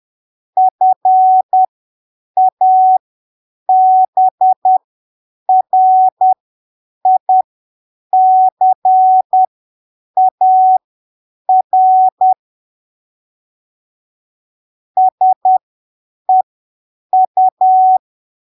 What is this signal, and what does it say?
Morse code, heard as a keyed tone, about 10 wpm, spaced as for 7 wpm, 750 Hz. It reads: FABRICAR SEU
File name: arquivos / texto / texto 20 10ppm